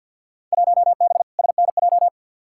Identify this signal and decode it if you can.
Morse code, heard as a keyed tone, about 47 wpm, 700 Hz. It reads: 1B SNJ
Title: Field Day Exchanges - 50wpm